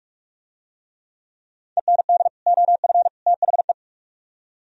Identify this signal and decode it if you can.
Morse code, heard as a keyed tone, about 45 wpm, 700 Hz. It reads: END OF THE